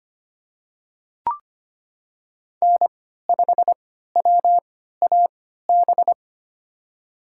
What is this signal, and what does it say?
Morse code, heard as a keyed tone, about 25 wpm, spaced as for 14 wpm, 700 Hz. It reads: N5WAB
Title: USA Call Signs - 5 Characters - 3x Character Spacing - 25wpm